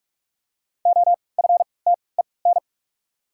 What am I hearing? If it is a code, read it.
Morse code, heard as a keyed tone, about 42 wpm, spaced as for 25 wpm, 700 Hz. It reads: OFTEN
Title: 3x Character Spacing - Top 300 Words - 45wpm